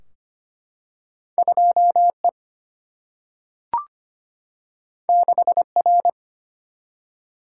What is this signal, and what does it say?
Morse code, heard as a keyed tone, about 25 wpm, 700 Hz. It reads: 2 E 6 R
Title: Single-Letter Single-Number 25wpm